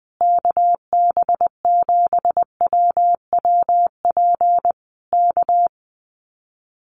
Morse code, heard as a keyed tone, about 20 wpm, 700 Hz. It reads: KB7WWP K
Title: Common QSO Phrases Encore 20wpm